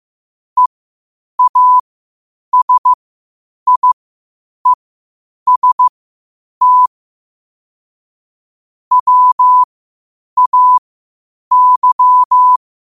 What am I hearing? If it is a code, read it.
Morse code, heard as a keyed tone, about 15 wpm, spaced as for 8 wpm, 1000 Hz. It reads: EASIEST WAY
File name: qotdfiles / 8wpm